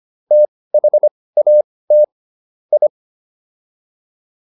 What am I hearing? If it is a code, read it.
Morse code, heard as a keyed tone, about 25 wpm, spaced as for 18 wpm, 600 Hz. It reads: THAT I